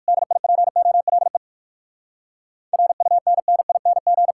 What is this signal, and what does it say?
Morse code, heard as a keyed tone, about 59 wpm, 700 Hz. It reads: DIPOLE RUNNING